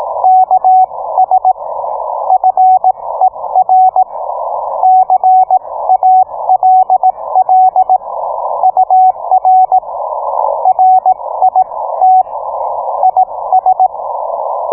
Morse code, heard as a keyed tone, about 18 wpm, spaced as for 15 wpm, 750 Hz. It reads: KS FER CALL UR RIT IS